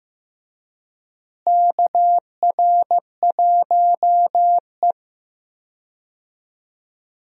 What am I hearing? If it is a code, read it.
Morse code, heard as a keyed tone, about 15 wpm, 700 Hz. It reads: KR1E
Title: USA Call Signs - 4 Characters - Repeated 2x - 15wpm